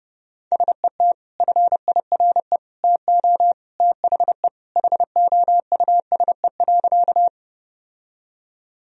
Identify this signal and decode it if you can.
Morse code, heard as a keyed tone, about 32 wpm, 700 Hz. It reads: SET FIRE TO THE HOUSE.
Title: Sentences from Top 1000 Words - Review Part 1 - 30wpm